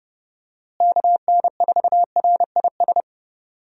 Morse code, heard as a keyed tone, about 30 wpm, 700 Hz. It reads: KN4RIS
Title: USA Call Signs - 6 Characters - Repeated 3x - 30wpm